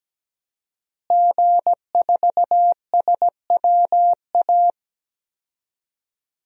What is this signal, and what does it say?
Morse code, heard as a keyed tone, about 17 wpm, 700 Hz. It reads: G4SWA
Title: DXCC-100 Call Signs - 5 Characters - Repeated 3x - 17wpm